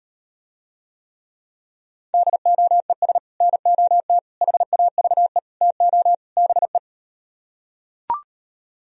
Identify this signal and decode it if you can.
Morse code, heard as a keyed tone, about 38 wpm, 700 Hz. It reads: DOES NOT HAVE TO BE